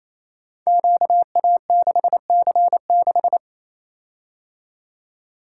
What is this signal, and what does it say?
Morse code, heard as a keyed tone, about 29 wpm, 700 Hz. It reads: QA6C6
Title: Code Groups - 5 Letter:Numbers - 28wpm